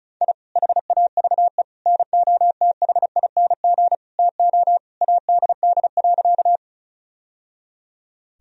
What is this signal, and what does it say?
Morse code, heard as a keyed tone, about 34 wpm, 700 Hz. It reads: I HAVE NOTHING TO ADD.
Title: Sentences from Top 1000 Words - Review Part 1 - 35wpm